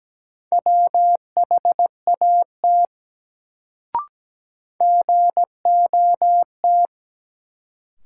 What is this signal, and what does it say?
Morse code, heard as a keyed tone, about 17 wpm, 700 Hz. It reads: WHAT GOT
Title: Sets of 2 Words - New Word Drill - 17wpm